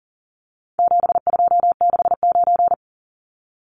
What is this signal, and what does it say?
Morse code, heard as a keyed tone, about 40 wpm, 700 Hz. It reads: 7269
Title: Numbers - 4-Digit 40wpm